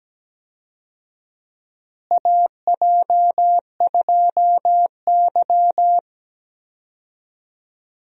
Morse code, heard as a keyed tone, about 17 wpm, 700 Hz. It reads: AJ2Y
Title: USA Call Signs - 4 Characters - Repeated 3x - 17wpm